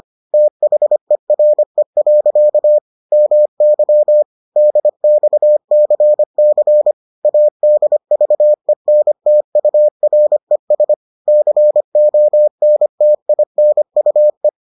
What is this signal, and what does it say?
Morse code, heard as a keyed tone, about 25 wpm, 600 Hz. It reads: THERE. MY DXCC ADVENTURES CONTINUE